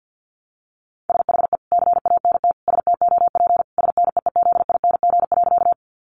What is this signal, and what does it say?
Morse code, heard as a keyed tone, about 50 wpm, 700 Hz. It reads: SHE CANT STOP SNEEZING.
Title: Easy Phrases 50wpm